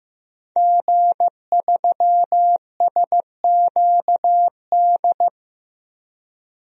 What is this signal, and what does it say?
Morse code, heard as a keyed tone, about 15 wpm, 700 Hz. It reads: G3SQD
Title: DXCC-100 Call Signs - 5 Characters - Repeated 3x - 15wpm